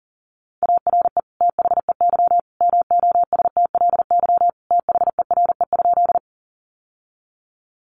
Morse code, heard as a keyed tone, about 40 wpm, 700 Hz. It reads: ARE THEY MOSTLY THERE?
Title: Sentences from Top 1300 Words 40wpm